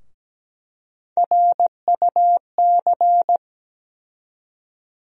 Morse code, heard as a keyed tone, about 17 wpm, 700 Hz. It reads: RUC
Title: USA Call Sign Postfixes - 3 Letters 17wpm